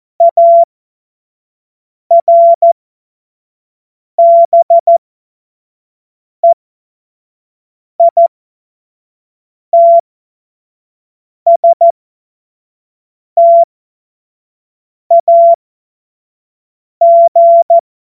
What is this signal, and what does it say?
Morse code, heard as a keyed tone, about 14 wpm, spaced as for 5 wpm, 680 Hz. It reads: ARBEITSTAG